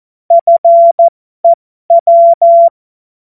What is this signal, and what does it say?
Morse code, heard as a keyed tone, about 14 wpm, spaced as for 12 wpm, 680 Hz. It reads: FEW
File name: Morsekurs / Witze